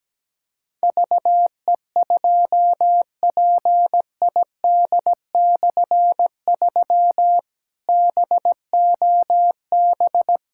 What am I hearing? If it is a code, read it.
Morse code, heard as a keyed tone, about 17 wpm, 700 Hz. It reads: VE2PID/3 BOB
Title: K1USN SST QSO Practice - 17wpm